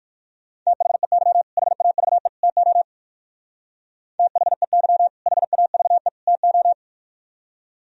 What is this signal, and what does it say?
Morse code, heard as a keyed tone, about 55 wpm, 700 Hz. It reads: THEY HAVE TO THEY HAVE TO
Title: Speed-Racing - Sets of 3 Words 45wpm